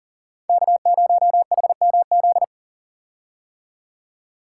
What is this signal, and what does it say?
Morse code, heard as a keyed tone, about 40 wpm, 700 Hz. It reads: K0HMZ
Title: K1USN SST QSO Practice - 40wpm